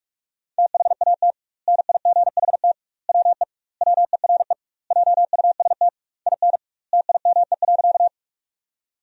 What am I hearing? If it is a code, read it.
Morse code, heard as a keyed tone, about 41 wpm, 700 Hz. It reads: THAT NIGHT WE WERE JUST IN TIME.
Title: 2x Word-Spacing - Sentences from Top 300 Words - 45wpm